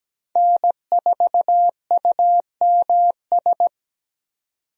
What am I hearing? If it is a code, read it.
Morse code, heard as a keyed tone, about 17 wpm, 700 Hz. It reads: N4UMS
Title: USA Call Signs - 5 Characters - 17wpm